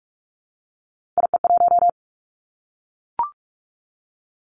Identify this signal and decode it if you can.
Morse code, heard as a keyed tone, about 46 wpm, 700 Hz. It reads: IE1
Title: Three Challenging Letter-Number Combinations 45wpm